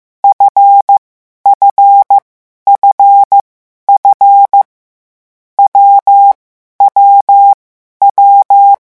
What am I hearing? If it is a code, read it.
Morse code, heard as a keyed tone, about 15 wpm, 800 Hz. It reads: F F F F W W W
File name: morse mp3 / lecon7-vitesse 14